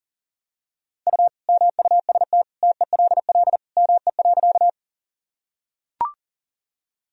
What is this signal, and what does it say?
Morse code, heard as a keyed tone, about 40 wpm, 700 Hz. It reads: U MUST TELL ME.
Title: Sentences from Top 200 Words and Abbreviations 40wpm